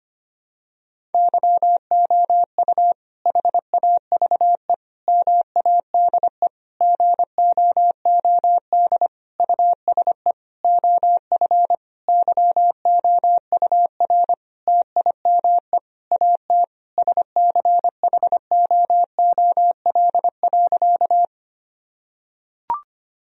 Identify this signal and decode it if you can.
Morse code, heard as a keyed tone, about 25 wpm, 700 Hz. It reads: YOU HAVE MADE GOOD USE OF YOUR TIME AT SCHOOL.
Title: Sentences from Top 1000 Words - Review Part 1 - 25wpm